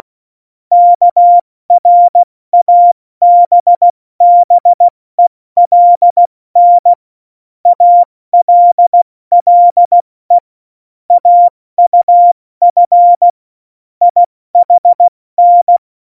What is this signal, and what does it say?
Morse code, heard as a keyed tone, about 16 wpm, 700 Hz. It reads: KRABBELN ALLE AUF IHN
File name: Morsekurs / Witze / Witz 03.Juli 201, 70 F80